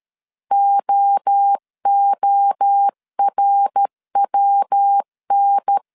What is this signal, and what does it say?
Morse code, heard as a keyed tone, about 13 wpm, 800 Hz. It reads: OORWN
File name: CW / DET / 1 3 1500C S B 12wpm 2828